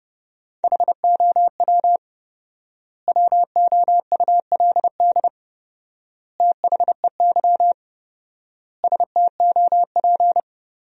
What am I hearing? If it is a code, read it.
Morse code, heard as a keyed tone, about 32 wpm, 700 Hz. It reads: HOW WOULD THEY STOP
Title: 4x Word-Spacing - Sentences from Top 300 Words - 30wpm